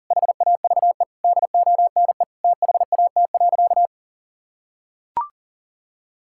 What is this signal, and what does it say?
Morse code, heard as a keyed tone, about 37 wpm, 700 Hz. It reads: HAVE DONE THAT.
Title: Sentences from Top 1600 Words 40wpm